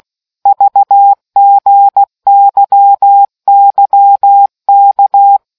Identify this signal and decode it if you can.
Morse code, heard as a keyed tone, about 16 wpm, 800 Hz. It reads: VGYYK